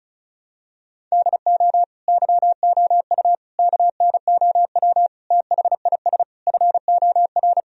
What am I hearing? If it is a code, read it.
Morse code, heard as a keyed tone, about 34 wpm, 700 Hz. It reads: DO YOU KNOW THIS FOR
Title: Sentences from Top 1000 Words - Review Part 2 - 35wpm